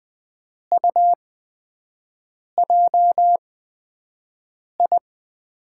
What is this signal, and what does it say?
Morse code, heard as a keyed tone, about 20 wpm, spaced as for 5 wpm, 700 Hz. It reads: UJI
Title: USA Call Signs - 8x Character Spacing - 20wpm